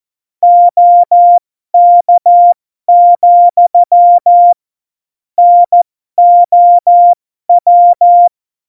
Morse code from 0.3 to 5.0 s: OK,